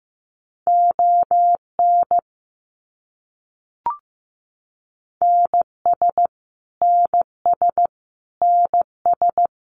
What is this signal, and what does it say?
Morse code, heard as a keyed tone, about 15 wpm, 700 Hz. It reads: ON NS NS NS
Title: Canadian Provinces and Territories - Repeated 3x - 15wpm